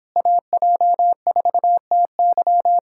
Morse code, heard as a keyed tone, about 26 wpm, 700 Hz. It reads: AJ4TY